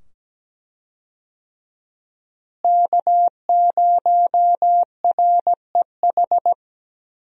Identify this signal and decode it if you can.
Morse code, heard as a keyed tone, about 17 wpm, 700 Hz. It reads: K0REH
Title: USA Call Signs - 5 Characters - Repeated 2x - 17wpm